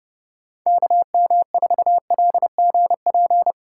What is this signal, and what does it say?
Morse code, heard as a keyed tone, about 30 wpm, 700 Hz. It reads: KM4LGP
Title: USA Call Signs - 6 Characters - 4x Character Spacing - 30wpm